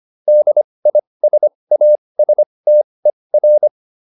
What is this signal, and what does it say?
Morse code, heard as a keyed tone, about 25 wpm, spaced as for 20 wpm, 600 Hz. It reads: DISASTER